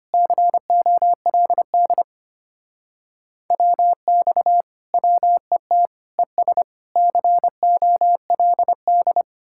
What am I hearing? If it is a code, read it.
Morse code, heard as a keyed tone, about 33 wpm, 700 Hz. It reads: COLD WX WET ES COLD